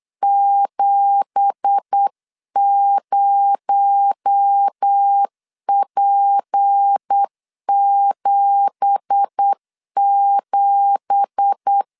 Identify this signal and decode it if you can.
Morse code, heard as a keyed tone, about 8 wpm, 800 Hz. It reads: 70P77
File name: LEZIONI / 9 2867 8wpm 3440